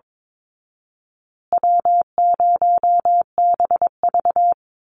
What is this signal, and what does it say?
Morse code, heard as a keyed tone, about 22 wpm, 700 Hz. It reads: W0BV
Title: Easy Call Signs Repeated 3x - 22wpm